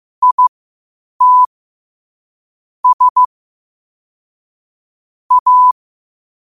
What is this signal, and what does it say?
Morse code, heard as a keyed tone, about 15 wpm, spaced as for 8 wpm, 1000 Hz. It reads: IT S A